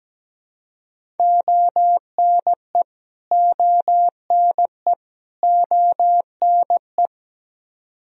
Morse code, heard as a keyed tone, about 17 wpm, 700 Hz. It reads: ONE ONE ONE
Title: Top 100 Words Repeated 17wpm